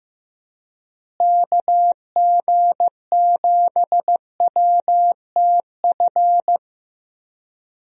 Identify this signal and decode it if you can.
Morse code, heard as a keyed tone, about 15 wpm, 700 Hz. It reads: KG7WTF
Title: USA Call Signs - 6 Characters - Repeated 2x - 15wpm